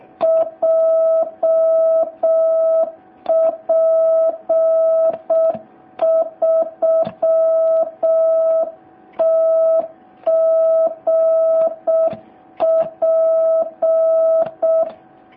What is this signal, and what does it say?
Morse code, heard as a keyed tone, about 6 wpm, 650 Hz. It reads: JP3TGP